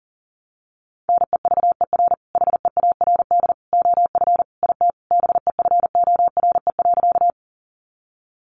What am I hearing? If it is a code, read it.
Morse code, heard as a keyed tone, about 40 wpm, 700 Hz. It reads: NEVER HEARD OF IT BEFORE.
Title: Sentences from Top 400 Words 40wpm